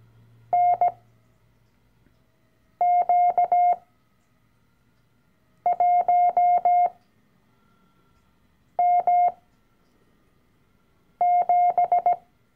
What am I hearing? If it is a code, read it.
Morse code, heard as a keyed tone, about 17 wpm, spaced as for 4 wpm, 710 Hz. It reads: NQ1M7
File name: CW-Online-Kurs / Anfänger / Lektion 012